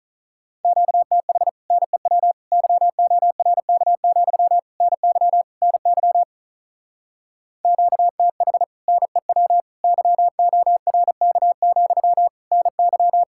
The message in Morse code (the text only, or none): QTH NEW YORK, NY NY QTH NEW YORK, NY